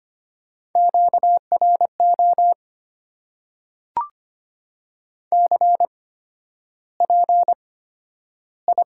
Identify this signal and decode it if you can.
Morse code, heard as a keyed tone, about 25 wpm, spaced as for 11 wpm, 690 Hz. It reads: QRO CPI
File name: Top 90 QSO Elements - 8x Character Spacing - 25wpm